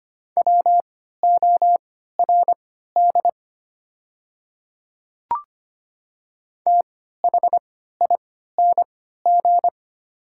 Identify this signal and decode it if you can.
Morse code, heard as a keyed tone, about 25 wpm, spaced as for 14 wpm, 700 Hz. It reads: WORD THING
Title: 3x Character Spacing - Top 100 Words - 25wpm